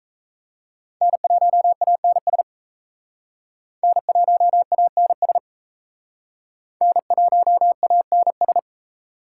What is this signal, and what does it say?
Morse code, heard as a keyed tone, about 36 wpm, 700 Hz. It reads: N1ANS N1ANS N1ANS